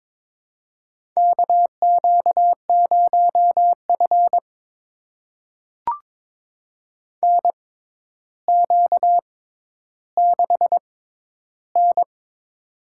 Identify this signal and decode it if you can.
Morse code, heard as a keyed tone, about 21 wpm, spaced as for 11 wpm, 700 Hz. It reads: KQ0F NQ6N
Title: USA Call Signs - 4 Characters - 6x Character Spacing - 22wpm